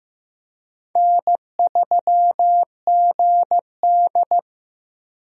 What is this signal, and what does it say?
Morse code, heard as a keyed tone, about 15 wpm, 700 Hz. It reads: N3GD
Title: USA Call Signs - 4 Characters - Repeated 3x - 15wpm